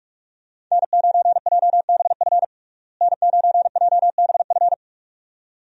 Morse code, heard as a keyed tone, about 42 wpm, 700 Hz. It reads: N9JBF N9JBF
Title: USA Call Signs - 5 Characters - Repeated 2x - 45wpm